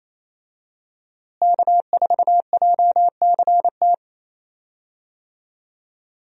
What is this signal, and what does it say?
Morse code, heard as a keyed tone, about 28 wpm, 700 Hz. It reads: K4JCT
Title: USA Call Signs - 5 Characters - Repeated 3x - 28wpm